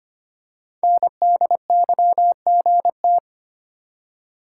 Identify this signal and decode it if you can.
Morse code, heard as a keyed tone, about 25 wpm, 690 Hz. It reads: NDYGT
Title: Code Groups - 5 Letter:Numbers - 25wpm